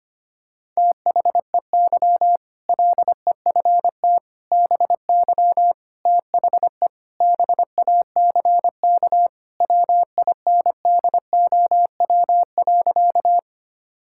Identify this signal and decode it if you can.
Morse code, heard as a keyed tone, about 25 wpm, 700 Hz. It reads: THEY LEFT BY THE BACK WINDOW.
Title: Sentences from Top 1600 Words 25wpm